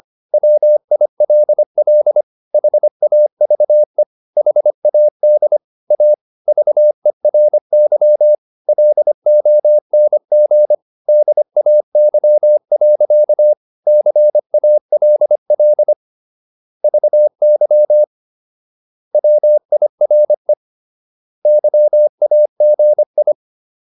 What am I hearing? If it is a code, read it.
Morse code, heard as a keyed tone, about 25 wpm, 600 Hz. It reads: WILL HAVE HAD A VERY LONG DAY. CALL VY WIRE YAGI